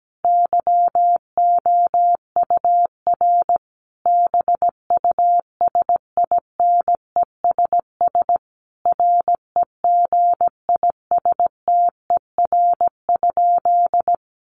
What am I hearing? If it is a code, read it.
Morse code, heard as a keyed tone, about 17 wpm, 700 Hz. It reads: YOUR BUSINESS REGISTER?
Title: Sentences from Top 1200 Words 17wpm